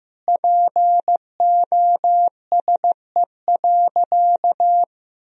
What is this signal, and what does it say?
Morse code, heard as a keyed tone, about 15 wpm, 700 Hz. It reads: POSE.
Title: Sentences from Top 1000 Words - Review Part 2 - 15wpm